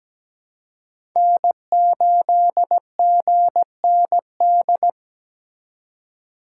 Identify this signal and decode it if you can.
Morse code, heard as a keyed tone, about 20 wpm, 700 Hz. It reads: N8GND
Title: Speed Racing - USA 1x3 Call Signs 17wpm